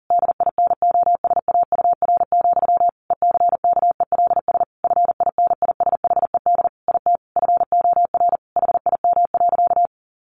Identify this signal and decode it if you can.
Morse code, heard as a keyed tone, about 40 wpm, 700 Hz. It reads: DINOSAUR, ECKELS FINISHED IT FOR HIM.
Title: A Sound Of Thunder 40wpm